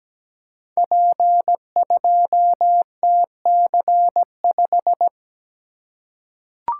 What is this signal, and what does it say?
Morse code, heard as a keyed tone, about 17 wpm, 700 Hz. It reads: P2TC5
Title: Code Groups - 5 Letter:Numbers - 17wpm